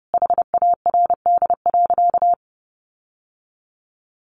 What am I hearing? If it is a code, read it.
Morse code, heard as a keyed tone, about 28 wpm, 700 Hz. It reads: HARD.
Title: Sentences from Top 1300 Words 30wpm